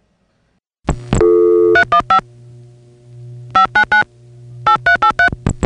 DTMF, heard as telephone touch tones, tones of about 100 ms. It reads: 315 599 8383